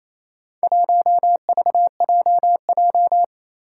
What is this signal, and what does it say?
Morse code, heard as a keyed tone, about 28 wpm, 700 Hz. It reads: 1VJJ
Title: Four Challenging Letter-Number Combinations 28wpm